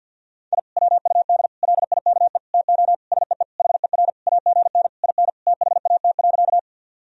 Morse code, heard as a keyed tone, about 51 wpm, 700 Hz. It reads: I WUD LIKE TO SEE HER AGN IN THAT.
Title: Sentences from Top 200 Words and Abbreviations 50wpm